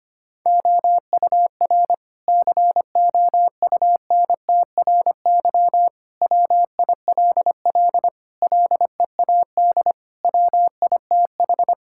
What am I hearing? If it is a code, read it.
Morse code, heard as a keyed tone, about 25 wpm, 700 Hz. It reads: OUR COUNTRY WILL LEAD WITH
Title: Sentences from Top 1000 Words - Review Part 5 - 25wpm